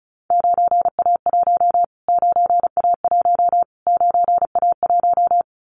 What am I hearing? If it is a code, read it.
Morse code, heard as a keyed tone, about 34 wpm, 700 Hz. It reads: 9A1 9A1 9A1
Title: Callsign Prefixes for Top 25 DXCC - Repeated 3x - 35wpm